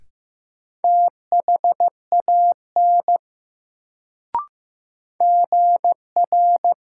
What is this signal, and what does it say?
Morse code, heard as a keyed tone, about 15 wpm, 700 Hz. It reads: THAN GR
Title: Sets of 2 Words - New Word Drill - 15wpm